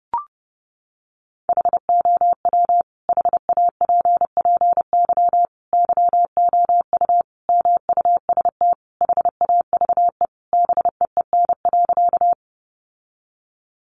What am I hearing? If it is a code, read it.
Morse code, heard as a keyed tone, about 30 wpm, 700 Hz. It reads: HOW HAPPY YOU MUST HAVE BEEN.
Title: Sentences from Top 1000 Words 30wpm